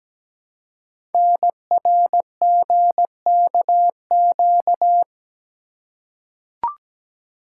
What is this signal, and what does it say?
Morse code, heard as a keyed tone, about 17 wpm, 700 Hz. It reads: NRGKQ
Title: Code Groups - 5 Letters - 17wpm